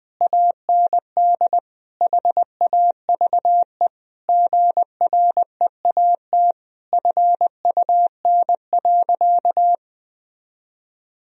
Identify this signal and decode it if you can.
Morse code, heard as a keyed tone, about 20 wpm, 700 Hz. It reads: AND HAVE GREAT FUN.
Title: Sentences from Top 1000 Words - Review Part 4 - 20wpm